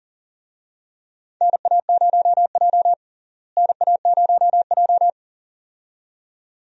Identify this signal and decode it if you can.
Morse code, heard as a keyed tone, about 39 wpm, 700 Hz. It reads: NA0J NA0J
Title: USA Call Signs - 4 Characters - Repeated 2x - 40wpm